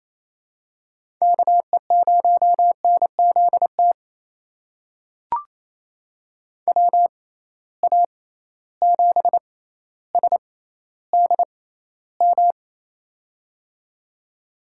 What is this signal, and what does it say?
Morse code, heard as a keyed tone, about 27 wpm, spaced as for 14 wpm, 700 Hz. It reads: KE0NZT WA7SDM
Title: USA Call Signs - 6 Characters - 6x Character Spacing - 28wpm